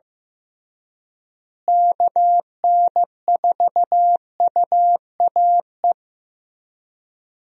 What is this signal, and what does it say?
Morse code, heard as a keyed tone, about 15 wpm, 700 Hz. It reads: KN4UAE
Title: USA Call Signs - 6 Characters - Repeated 3x - 15wpm